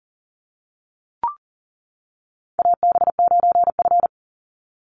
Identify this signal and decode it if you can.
Morse code, heard as a keyed tone, about 40 wpm, 700 Hz. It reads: AB9F
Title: USA Call Signs 40wpm